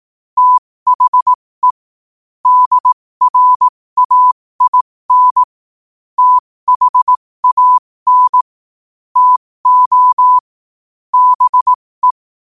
Morse code, heard as a keyed tone, about 18 wpm, spaced as for 15 wpm, 1000 Hz. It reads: THE DRAIN THAN TO BE
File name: qotdfiles / 15wpm